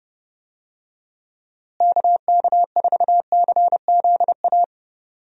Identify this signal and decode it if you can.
Morse code, heard as a keyed tone, about 30 wpm, 700 Hz. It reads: KK4CZA
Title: USA Call Signs - 6 Characters - Repeated 3x - 30wpm